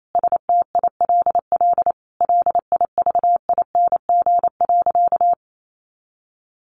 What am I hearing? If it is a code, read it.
Morse code, heard as a keyed tone, about 28 wpm, 700 Hz. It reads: STILL LIVING.
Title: Sentences from Top 1100 Words 28wpm